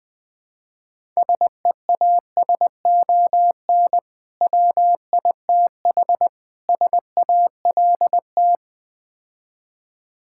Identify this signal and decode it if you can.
Morse code, heard as a keyed tone, about 21 wpm, 700 Hz. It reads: SEASON WITH SALT
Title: Build A Phrase - 5-Word Phrases - 20wpm